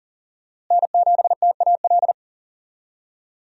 Morse code, heard as a keyed tone, about 40 wpm, 700 Hz. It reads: N7TAL
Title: USA Call Signs - 5 Characters - Repeated 2x - 40wpm